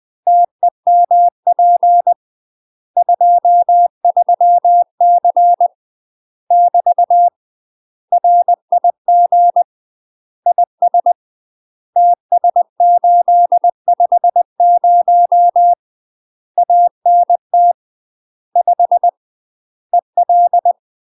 Morse code, heard as a keyed tone, about 20 wpm, 700 Hz. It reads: TEMP 23C <BT> RIG IS TS850 ANT 5 EL
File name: UK MASTER QSOs TRAINING / 20WPM / UK MASTER QSOs 0069